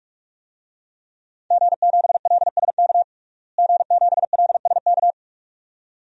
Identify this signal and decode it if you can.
Morse code, heard as a keyed tone, about 46 wpm, 700 Hz. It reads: G7LSK G7LSK
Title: DXCC-100 Call Signs - 5 Characters - Repeated 2x - 45wpm